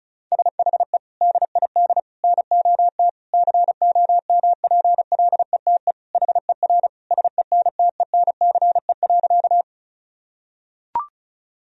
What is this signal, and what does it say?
Morse code, heard as a keyed tone, about 34 wpm, 700 Hz. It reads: SHE DID NOT COMPLETE HER SENTENCE.
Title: Sentences from Top 1000 Words - Review Part 2 - 35wpm